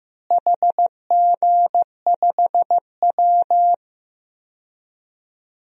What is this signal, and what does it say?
Morse code, heard as a keyed tone, about 15 wpm, 700 Hz. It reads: HG5W
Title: DXCC-100 Call Signs - 4 Characters - Repeated 2x - 15wpm